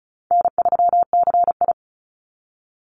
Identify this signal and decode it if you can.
Morse code, heard as a keyed tone, about 34 wpm, 700 Hz. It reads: N3CI
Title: Call Signs Repeated 2x - 35wpm